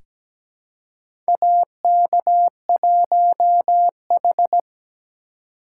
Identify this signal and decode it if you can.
Morse code, heard as a keyed tone, about 17 wpm, 700 Hz. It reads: AK1H
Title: USA Call Signs - 4 Characters - 4x Character Spacing - 17wpm